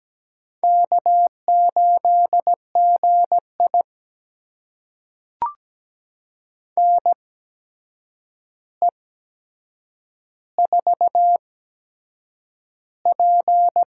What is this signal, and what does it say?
Morse code, heard as a keyed tone, about 17 wpm, spaced as for 7 wpm, 700 Hz. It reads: K8GI NE4P
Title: USA Call Signs - 4 Characters - 8x Character Spacing - 17wpm